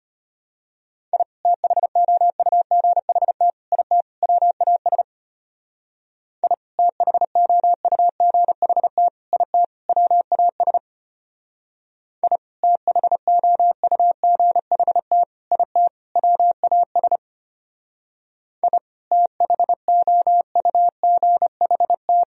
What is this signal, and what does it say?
Morse code, heard as a keyed tone, about 30 wpm, 700 Hz. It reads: I THOUGHT IT WAS I THOUGHT IT WAS I THOUGHT IT WAS I THOUGHT
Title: Speed-Racing - Sets of 4 Words 25wpm